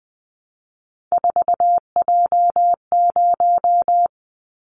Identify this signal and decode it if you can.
Morse code, heard as a keyed tone, about 20 wpm, 700 Hz. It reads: 4J0
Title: Callsign Prefixes for Top 100 DXCC - 20wpm